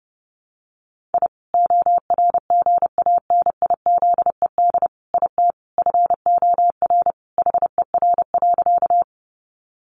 Morse code, heard as a keyed tone, about 30 wpm, 700 Hz. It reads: I ORGANIZED IT FOR HER.
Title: Sentences from Top 1300 Words 30wpm